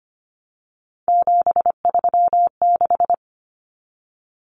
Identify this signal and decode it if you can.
Morse code, heard as a keyed tone, about 25 wpm, 700 Hz. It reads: 736
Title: Common Cut Numbers - 3-Digit 25wpm